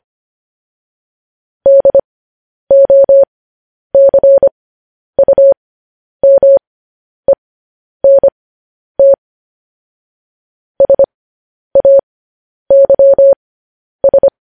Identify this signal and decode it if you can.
Morse code, heard as a keyed tone, about 25 wpm, spaced as for 10 wpm, 550 Hz. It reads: DOCUMENT SAYS